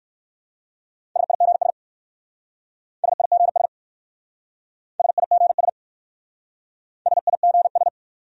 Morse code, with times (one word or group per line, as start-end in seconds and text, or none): 1.2-1.7 SIGS
3.0-3.7 SIGS
5.0-5.7 SIGS
7.1-7.9 SIGS